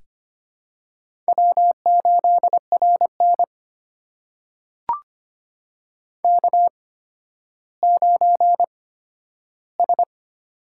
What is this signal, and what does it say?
Morse code, heard as a keyed tone, about 25 wpm, spaced as for 12 wpm, 700 Hz. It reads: W8RN K9S
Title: USA Call Signs - 4 Characters - 8x Character Spacing - 25wpm